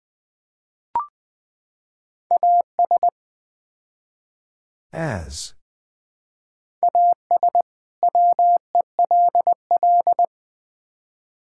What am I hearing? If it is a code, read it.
Morse code, heard as a keyed tone, about 20 wpm, 700 Hz. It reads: AS AS WELL
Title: Build A Phrase - 4-Word Phrases - 20wpm